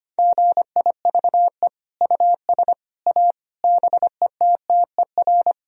Morse code, heard as a keyed tone, about 25 wpm, 700 Hz. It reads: GIVE US A BETTER
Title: Sentences from Top 1000 Words - Review Part 4 - 25wpm